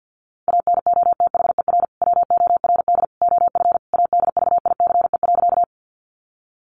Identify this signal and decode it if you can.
Morse code, heard as a keyed tone, about 50 wpm, 700 Hz. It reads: ANOTHER WORD OF ADVICE.
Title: Sentences from Top 1100 Words 50wpm